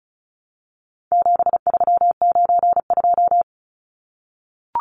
Morse code, heard as a keyed tone, about 34 wpm, 700 Hz. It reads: 7392
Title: Numbers - 4-Digit 35wpm